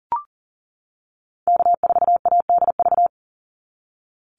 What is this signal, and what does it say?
Morse code, heard as a keyed tone, about 40 wpm, 700 Hz. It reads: K4ADV
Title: NA SOTA Activator Callsigns 40wpm